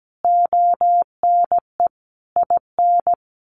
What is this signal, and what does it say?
Morse code, heard as a keyed tone, about 17 wpm, 700 Hz. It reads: ONE IN
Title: A Sound Of Thunder 17wpm